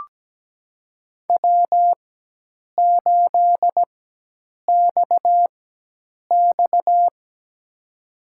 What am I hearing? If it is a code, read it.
Morse code, heard as a keyed tone, about 17 wpm, spaced as for 8 wpm, 700 Hz. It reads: W8XX
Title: USA Call Signs - 4 Characters - 4x Character Spacing - 17wpm